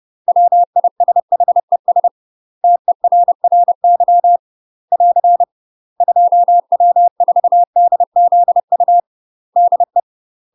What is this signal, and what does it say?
Morse code, heard as a keyed tone, about 30 wpm, 700 Hz. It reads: WISHES TERRY <AR> 2W4DZU DE
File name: UK MASTER QSOs TRAINING / 30WPM / UK MASTER QSOs 0045